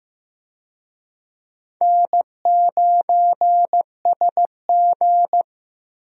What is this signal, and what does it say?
Morse code, heard as a keyed tone, about 15 wpm, 700 Hz. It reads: N9SG
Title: USA Call Signs - 4 Characters - Repeated 2x - 15wpm